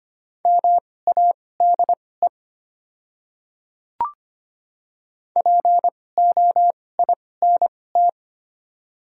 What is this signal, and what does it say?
Morse code, heard as a keyed tone, about 25 wpm, spaced as for 18 wpm, 700 Hz. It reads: MADE POINT
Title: 2x Character Spacing - Top 200 Words - 25wpm